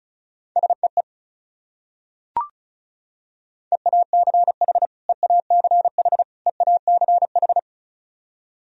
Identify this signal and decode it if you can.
Morse code, heard as a keyed tone, about 34 wpm, 700 Hz. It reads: SEE EACH EACH EACH
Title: Top 100 Words Repeated 35wpm